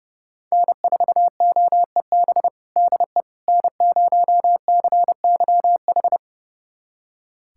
Morse code, heard as a keyed tone, about 30 wpm, 700 Hz. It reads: N4OEB DE N0CYH
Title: USA Callsign DE Callsign 30wpm